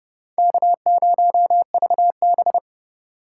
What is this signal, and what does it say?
Morse code, heard as a keyed tone, about 31 wpm, 700 Hz. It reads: K0VB
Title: USA Call Signs - 2x Character Spacing - 30wpm